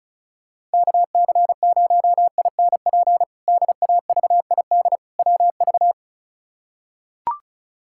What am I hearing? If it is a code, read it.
Morse code, heard as a keyed tone, about 36 wpm, 700 Hz. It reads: KC0INP DAVID WV
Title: K1USN SST QSO Practice - 35wpm